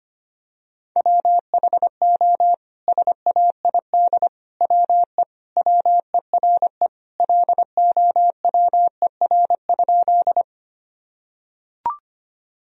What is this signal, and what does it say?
Morse code, heard as a keyed tone, about 25 wpm, 700 Hz. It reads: WHO SAID WE WERE LOWER?
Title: Sentences from Top 1000 Words - Review Part 4 - 25wpm